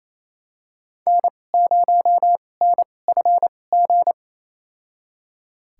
Morse code, heard as a keyed tone, about 28 wpm, spaced as for 20 wpm, 700 Hz. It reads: N0NFG
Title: USA Call Signs - 5 Characters - 2x Character Spacing - 28wpm